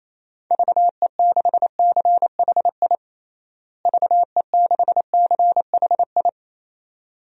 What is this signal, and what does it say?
Morse code, heard as a keyed tone, about 28 wpm, 700 Hz. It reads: VE6CHI VE6CHI
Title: DXCC-100 Call Signs - 6 Characters - Repeated 2x - 28wpm